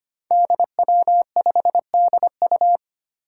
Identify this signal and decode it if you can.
Morse code, heard as a keyed tone, about 25 wpm, 700 Hz. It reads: DW5DU
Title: Five Challenging Letter-Number Combinations 25wpm